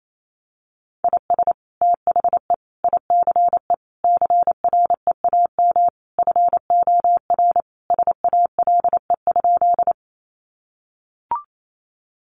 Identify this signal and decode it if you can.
Morse code, heard as a keyed tone, about 28 wpm, 700 Hz. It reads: IS THE ICE CREAM FOR SALE?
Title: Sentences from Top 1500 Words 28wpm